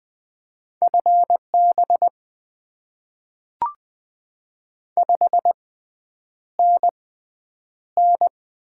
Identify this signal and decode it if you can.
Morse code, heard as a keyed tone, about 20 wpm, spaced as for 9 wpm, 700 Hz. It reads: FB 5NN
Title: Top 20 QSO Elements - 6x Character Spacing - 20wpm